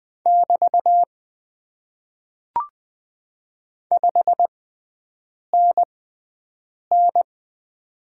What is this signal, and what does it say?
Morse code, heard as a keyed tone, about 20 wpm, spaced as for 7 wpm, 700 Hz. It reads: <BT> 5NN
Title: Top 90 QSO Elements - 6x Character Spacing - 20wpm